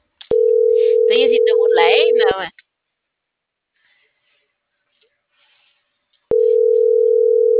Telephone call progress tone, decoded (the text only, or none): ringback tone